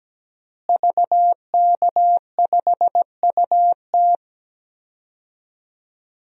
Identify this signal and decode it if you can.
Morse code, heard as a keyed tone, about 17 wpm, 700 Hz. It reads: VK5UT
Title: DXCC-100 Call Signs - 5 Characters - Repeated 2x - 17wpm